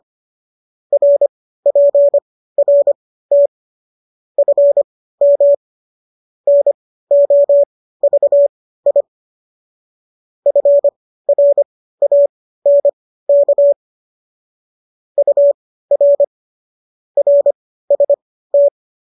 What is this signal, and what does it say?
Morse code, heard as a keyed tone, about 25 wpm, spaced as for 15 wpm, 570 Hz. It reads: RPRT FM NOVI FRANK UR RST